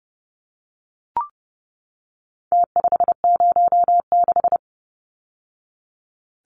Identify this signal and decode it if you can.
Morse code, heard as a keyed tone, about 30 wpm, 700 Hz. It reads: T506
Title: Numbers - Contest Style - 4-Digit 30wpm